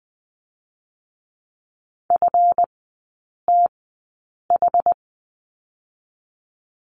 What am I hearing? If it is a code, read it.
Morse code, heard as a keyed tone, about 20 wpm, spaced as for 8 wpm, 700 Hz. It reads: FTH
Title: 198 Lesson 32e - Letter k - Three Characters 20wpm